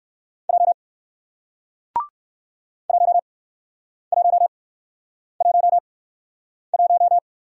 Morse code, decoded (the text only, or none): J 1111